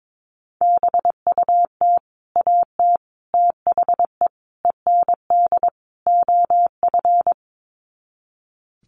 Morse code, heard as a keyed tone, about 22 wpm, 700 Hz. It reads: BUT AT THE END OF
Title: Sets of 5 Words Encore2 22wpm